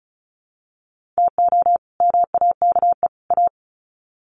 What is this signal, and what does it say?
Morse code, heard as a keyed tone, about 34 wpm, 700 Hz. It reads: TO MAKE A
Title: Sets of 3 Words 35wpm